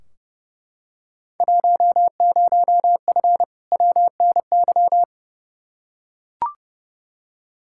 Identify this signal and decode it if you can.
Morse code, heard as a keyed tone, about 30 wpm, 700 Hz. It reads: 10F WNY
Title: Field Day Exchanges Encore - 30wpm